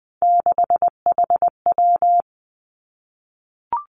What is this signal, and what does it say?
Morse code, heard as a keyed tone, about 20 wpm, 700 Hz. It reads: 6HW
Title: Three Challenging Letter-Number Combinations 20wpm